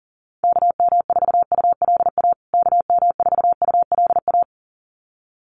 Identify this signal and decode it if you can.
Morse code, heard as a keyed tone, about 37 wpm, 700 Hz. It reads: KM4ULA KM4ULA
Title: USA Call Signs Repeated 2x 40wpm